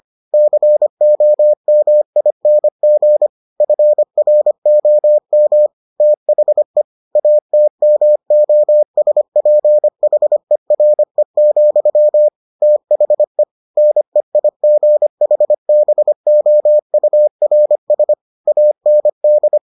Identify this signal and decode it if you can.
Morse code, heard as a keyed tone, about 25 wpm, 600 Hz. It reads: COMING FROM THE ATMOSPHERE, THE NEIGHBOURS AND